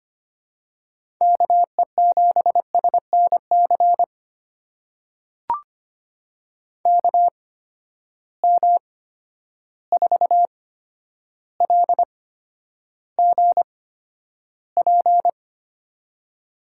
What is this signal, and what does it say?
Morse code, heard as a keyed tone, about 25 wpm, spaced as for 11 wpm, 700 Hz. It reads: KE7SNC KM4LGP
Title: USA Call Signs - 6 Characters - 8x Character Spacing - 25wpm